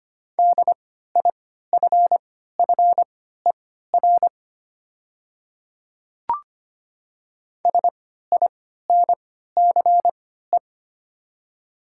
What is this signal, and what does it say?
Morse code, heard as a keyed tone, about 25 wpm, spaced as for 14 wpm, 690 Hz. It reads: DIFFER SINCE
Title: 3x Character Spacing - Top 500 Words - 25wpm